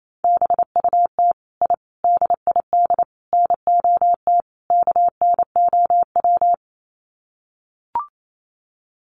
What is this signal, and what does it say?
Morse code, heard as a keyed tone, about 28 wpm, 700 Hz. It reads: BUT I DID NOT KNOW
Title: Sets of 5 Words Encore 28wpm